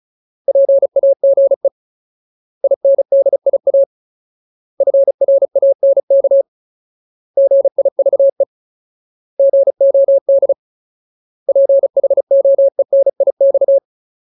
Morse code, heard as a keyed tone, about 36 wpm, 550 Hz. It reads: PAGE INDIA FRANK GIVE GOD PHOENIX
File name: Ultimate Word List / 35WPM Ultimate